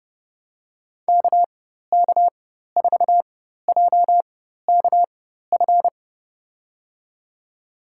Morse code, heard as a keyed tone, about 30 wpm, spaced as for 14 wpm, 700 Hz. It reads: KK4JKF
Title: USA Call Signs - 6 Characters - 4x Character Spacing - 30wpm